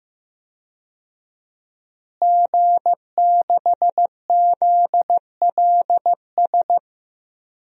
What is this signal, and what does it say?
Morse code, heard as a keyed tone, about 15 wpm, 700 Hz. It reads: G6ZLS
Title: DXCC-100 Call Signs - 5 Characters - Repeated 3x - 15wpm